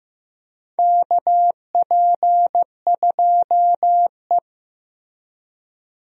Morse code, heard as a keyed tone, about 15 wpm, 700 Hz. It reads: KP2E
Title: DXCC-100 Call Signs - 4 Characters - Repeated 3x - 15wpm